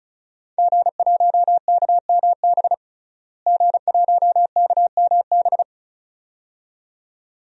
Morse code, heard as a keyed tone, about 36 wpm, 700 Hz. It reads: G1KMB G1KMB
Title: DXCC-100 Call Signs - 5 Characters - Repeated 2x - 35wpm